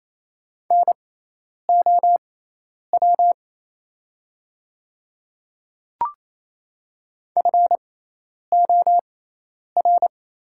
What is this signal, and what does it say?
Morse code, heard as a keyed tone, about 27 wpm, spaced as for 10 wpm, 700 Hz. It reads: NOW FOR
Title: 6x Character Spacing - Top 100 Words - 28wpm